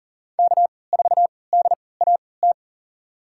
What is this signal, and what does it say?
Morse code, heard as a keyed tone, about 40 wpm, spaced as for 23 wpm, 700 Hz. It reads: K4DAT
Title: USA Call Signs - 5 Characters - 3x Character Spacing - 40wpm